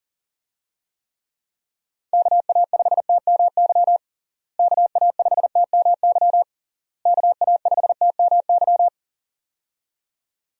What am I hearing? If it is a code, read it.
Morse code, heard as a keyed tone, about 40 wpm, 700 Hz. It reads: KA5TMY KA5TMY KA5TMY
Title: USA Call Signs - 6 Characters - Repeated 3x - 40wpm